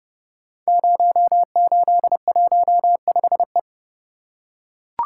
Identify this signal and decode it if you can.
Morse code, heard as a keyed tone, about 30 wpm, 700 Hz. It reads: 0815E
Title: Code Groups - 5 Letter:Numbers - 30wpm